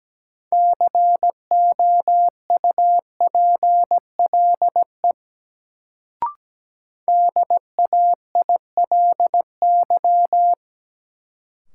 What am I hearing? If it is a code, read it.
Morse code, heard as a keyed tone, about 17 wpm, 700 Hz. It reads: COUPLE DAILY
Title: Sentences from Top 1100 Words - New Word Drill - 17wpm